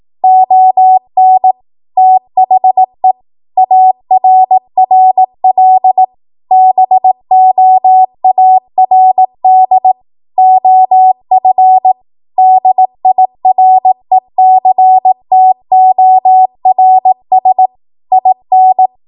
Morse code, heard as a keyed tone, about 18 wpm, 750 Hz. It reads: ON THE ARRL BOARD OF DIRECTORS IN